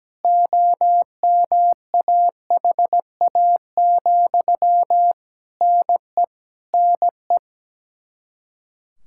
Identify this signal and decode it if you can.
Morse code, heard as a keyed tone, about 17 wpm, 700 Hz. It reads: OMAHA, NE NE
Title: Practice QSO - Exchange 1 - Beginning - 17wpm